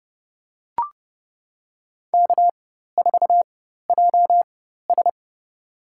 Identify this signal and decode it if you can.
Morse code, heard as a keyed tone, about 30 wpm, spaced as for 14 wpm, 700 Hz. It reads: K4JS
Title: USA Call Signs - 4 Characters - 4x Character Spacing - 30wpm